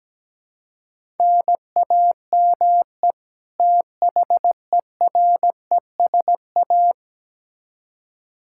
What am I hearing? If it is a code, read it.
Morse code, heard as a keyed tone, about 17 wpm, 700 Hz. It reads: NAME THERESA
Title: K1USN SST QSO Practice 17wpm